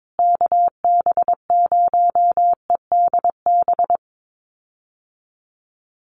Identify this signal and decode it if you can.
Morse code, heard as a keyed tone, about 22 wpm, 700 Hz. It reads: KB0EDB
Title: USA Call Signs Repeated 3x 22wpm